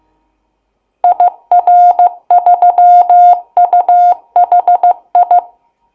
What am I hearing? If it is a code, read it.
Morse code, heard as a keyed tone, about 15 wpm, 710 Hz. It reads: IR3UHI
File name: Kennung-Gitschberg-CW